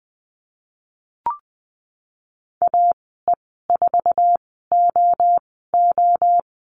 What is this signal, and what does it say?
Morse code, heard as a keyed tone, about 20 wpm, spaced as for 15 wpm, 700 Hz. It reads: AE4OO
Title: 069 Lesson 12g - Letter d - Call Signs 20wpm